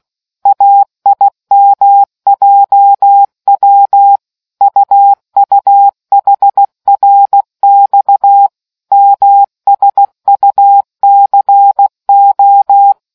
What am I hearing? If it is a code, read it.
Morse code, heard as a keyed tone, about 16 wpm, 800 Hz. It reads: AIMJW UUHRX MSUCO